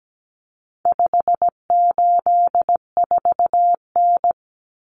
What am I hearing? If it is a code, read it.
Morse code, heard as a keyed tone, about 17 wpm, 700 Hz. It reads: 584N
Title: Common Cut Numbers - 4-Digit 17wpm